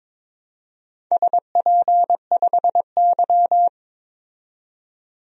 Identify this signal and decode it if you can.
Morse code, heard as a keyed tone, about 22 wpm, 700 Hz. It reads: SP5Y
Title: DXCC-100 Call Signs - 4 Characters - 22wpm